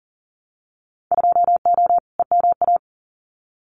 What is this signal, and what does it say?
Morse code, heard as a keyed tone, about 40 wpm, 700 Hz. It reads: 2O EMA
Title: Winter Field Day Exchanges 40wpm